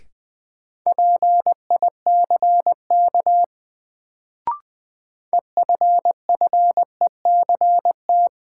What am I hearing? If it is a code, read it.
Morse code, heard as a keyed tone, about 20 wpm, 700 Hz. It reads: PICK EFFECT
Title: Sentences from Top 1000 Words - New Word Drill - 20wpm